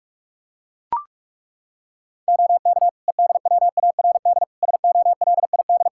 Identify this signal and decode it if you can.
Morse code, heard as a keyed tone, about 42 wpm, 700 Hz. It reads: OK EDWARD SOLID
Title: Practice QSO - Exchange 3 - More Info - 45wpm